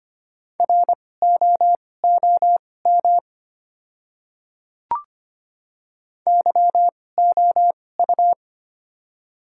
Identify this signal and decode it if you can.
Morse code, heard as a keyed tone, about 25 wpm, spaced as for 18 wpm, 700 Hz. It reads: ROOM YOU
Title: 2x Character Spacing - Top 300 Words - 25wpm